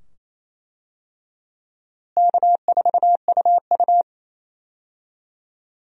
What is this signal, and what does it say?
Morse code, heard as a keyed tone, about 28 wpm, 700 Hz. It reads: K4UU
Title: USA Call Signs - 4 Characters - Repeated 2x - 28wpm